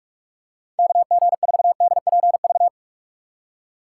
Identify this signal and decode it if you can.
Morse code, heard as a keyed tone, about 46 wpm, 700 Hz. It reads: KG4DPV
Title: USA Call Signs - 6 Characters - 2x Character Spacing - 45wpm